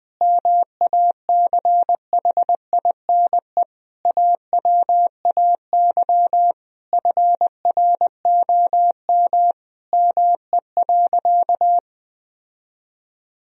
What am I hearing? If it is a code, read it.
Morse code, heard as a keyed tone, about 21 wpm, 700 Hz. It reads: MACHINE AWAY FROM ME.
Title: Sentences from Top 1000 Words - Review Part 2 - 20wpm